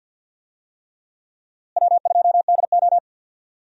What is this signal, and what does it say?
Morse code, heard as a keyed tone, about 47 wpm, 700 Hz. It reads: W2DO
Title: USA Call Signs - 4 Characters - Repeated 3x - 50wpm